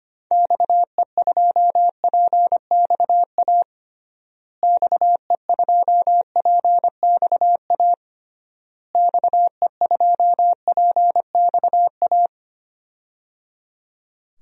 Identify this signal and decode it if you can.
Morse code, heard as a keyed tone, about 25 wpm, 700 Hz. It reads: XE2PXA XE2PXA XE2PXA
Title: DXCC-100 Call Signs - 6 Characters - Repeated 3x - 25wpm